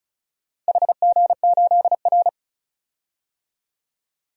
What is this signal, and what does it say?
Morse code, heard as a keyed tone, about 36 wpm, 700 Hz. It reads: HG8R
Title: DXCC-100 Call Signs - 4 Characters - 35wpm